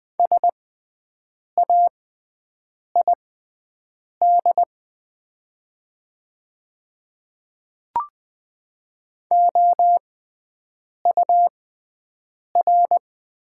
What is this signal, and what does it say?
Morse code, heard as a keyed tone, about 20 wpm, spaced as for 7 wpm, 700 Hz. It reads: SAID OUR
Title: 6x Character Spacing - Top 500 Words - 20wpm